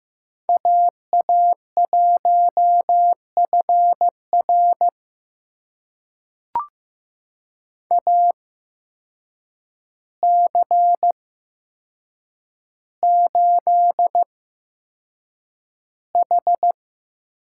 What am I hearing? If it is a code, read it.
Morse code, heard as a keyed tone, about 15 wpm, spaced as for 7 wpm, 700 Hz. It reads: AA1FR AC8H
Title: USA Call Signs - 5 Characters - 8x Character Spacing - 15wpm